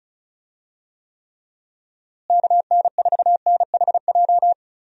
Morse code, heard as a keyed tone, about 36 wpm, 700 Hz. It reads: KN4NHJ
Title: USA Call Signs - 6 Characters - Repeated 2x - 35wpm